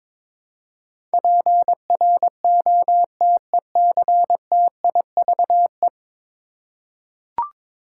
Morse code, heard as a keyed tone, about 22 wpm, 700 Hz. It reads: PROTECTIVE
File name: practice / 22wpm / 10 Letter Words - First 3 Letters Given - 22wpm